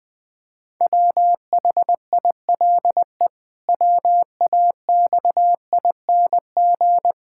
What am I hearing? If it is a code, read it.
Morse code, heard as a keyed tone, about 20 wpm, 700 Hz. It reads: WHILE WAXING
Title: Pangram Challenge v2 - 20wpm